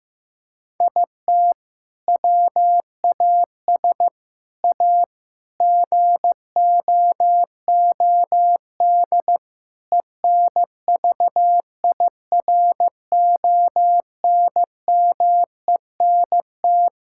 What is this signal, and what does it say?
Morse code, heard as a keyed tone, about 15 wpm, 700 Hz. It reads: IT WAS A GOOD ENVIRONMENT
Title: Sentences from Top 1000 Words - Review Part 5 - 15wpm